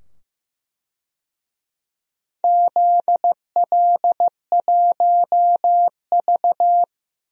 Morse code, heard as a keyed tone, about 15 wpm, 700 Hz. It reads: ZL1V
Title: DXCC-100 Call Signs - 4 Characters - 15wpm